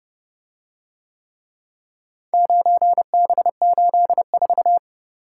Morse code, heard as a keyed tone, about 27 wpm, 700 Hz. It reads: 9B84